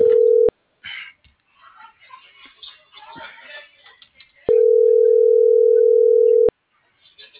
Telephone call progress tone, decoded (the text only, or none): ringback tone